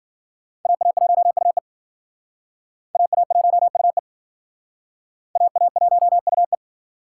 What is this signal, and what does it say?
Morse code, heard as a keyed tone, about 53 wpm, 700 Hz. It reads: AA1UE AA1UE AA1UE